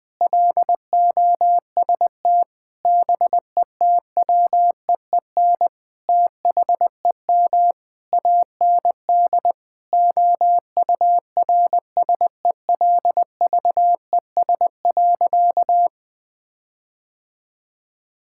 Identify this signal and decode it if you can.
Morse code, heard as a keyed tone, about 21 wpm, 700 Hz. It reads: LOST BETWEEN THEM AND OURSELVES.
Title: Sentences from Top 1600 Words 20wpm